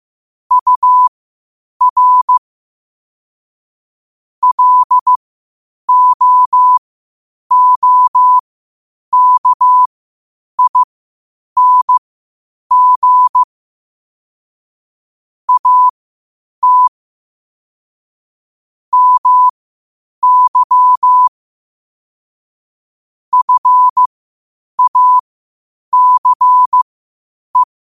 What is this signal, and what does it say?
Morse code, heard as a keyed tone, about 15 wpm, spaced as for 8 wpm, 1000 Hz. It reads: UR LOOKING AT MY FACE
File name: qotdfiles / 8wpm